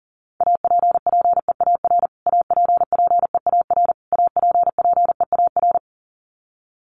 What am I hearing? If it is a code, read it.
Morse code, heard as a keyed tone, about 40 wpm, 700 Hz. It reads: APPEAR APPEAR APPEAR
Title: Top 500 Words Repeated 40wpm